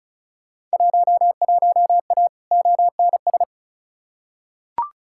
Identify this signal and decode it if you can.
Morse code, heard as a keyed tone, about 36 wpm, 700 Hz. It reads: 11A ONS
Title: Field Day Exchanges Encore - 35wpm